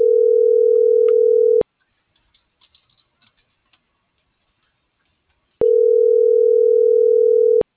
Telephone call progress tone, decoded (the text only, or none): ringback tone